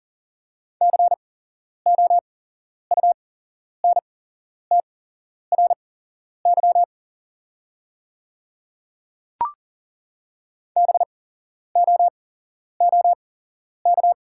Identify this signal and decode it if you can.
Morse code, heard as a keyed tone, about 40 wpm, spaced as for 11 wpm, 690 Hz. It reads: COUNTRY BOOK
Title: 8x Character Spacing - Top 300 Words - 40wpm